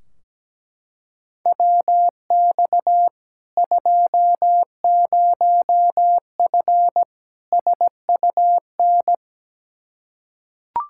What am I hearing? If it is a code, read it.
Morse code, heard as a keyed tone, about 17 wpm, 700 Hz. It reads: WX 20F SUN
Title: Practice QSO - Exchange 2 - Info - 17wpm